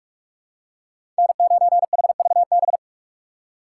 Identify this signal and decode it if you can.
Morse code, heard as a keyed tone, about 46 wpm, 700 Hz. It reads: N9HVB
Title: USA Call Signs - 5 Characters - 3x Character Spacing - 45wpm